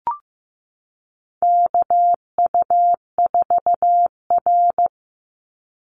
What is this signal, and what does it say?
Morse code, heard as a keyed tone, about 15 wpm, 700 Hz. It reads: KU4R
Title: NA SOTA Chaser Callsigns 15wpm